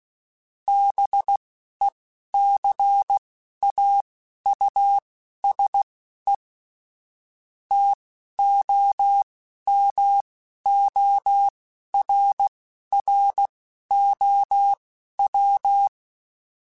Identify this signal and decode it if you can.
Morse code, heard as a keyed tone, about 16 wpm, spaced as for 11 wpm, 800 Hz. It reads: BECAUSE TOMORROW
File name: Chuckles / Chuckles-mp3s / SoftWords @10